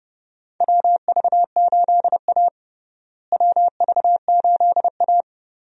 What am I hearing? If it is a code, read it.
Morse code, heard as a keyed tone, about 29 wpm, 700 Hz. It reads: WV8A WV8A
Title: USA Call Signs - 4 Characters - Repeated 3x - 30wpm